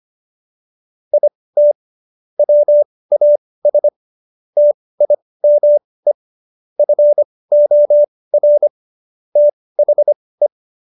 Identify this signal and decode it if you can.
Morse code, heard as a keyed tone, about 25 wpm, spaced as for 18 wpm, 590 Hz. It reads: IT WAS TIME FOR THE